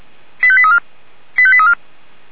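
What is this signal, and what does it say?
Five-tone selective call (ZVEI), heard as quick runs of five tones, tones of about 70 ms. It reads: 85834 85834
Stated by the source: ZVEI five-tone call 85863 received by SDR.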